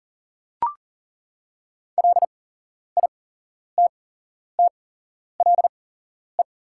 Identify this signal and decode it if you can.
Morse code, heard as a keyed tone, about 40 wpm, spaced as for 11 wpm, 700 Hz. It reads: LITTLE
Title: 8x Character Spacing - Top 200 Words - 40wpm